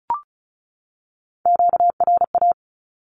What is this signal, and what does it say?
Morse code, heard as a keyed tone, about 34 wpm, 700 Hz. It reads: QRA
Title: Q-Codes 35wpm